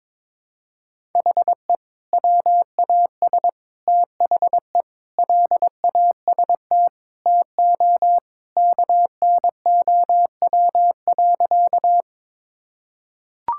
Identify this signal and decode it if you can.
Morse code, heard as a keyed tone, about 22 wpm, 700 Hz. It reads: HE WAS THE LAST TO KNOW.
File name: practice / 22wpm / Sentences with 2-Word Phrases - 22wpm